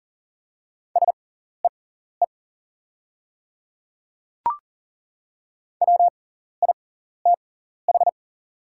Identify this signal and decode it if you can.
Morse code, heard as a keyed tone, about 40 wpm, spaced as for 14 wpm, 700 Hz. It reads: SEE WITH
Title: 6x Character Spacing - Top 200 Words - 40wpm